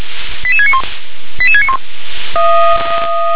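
Five-tone selective call (ZVEI), heard as quick runs of five tones, tones of about 70 ms. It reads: 88691 88691
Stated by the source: ZVEI five-tone call 88771 received by SDR.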